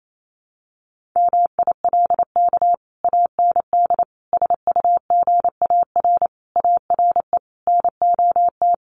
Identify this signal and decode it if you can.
Morse code, heard as a keyed tone, about 28 wpm, 700 Hz. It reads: MILK AND SUGAR ARE NOT
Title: Sentences from Top 1500 Words 28wpm